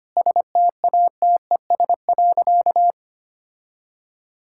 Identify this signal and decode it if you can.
Morse code, heard as a keyed tone, about 25 wpm, 700 Hz. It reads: STATES.
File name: practice / 25wpm / Sentences with 9-Word Phrases - 25wpm